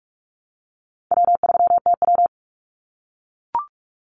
Morse code, heard as a keyed tone, about 44 wpm, 700 Hz. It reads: W3TW
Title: Call Signs - 45wpm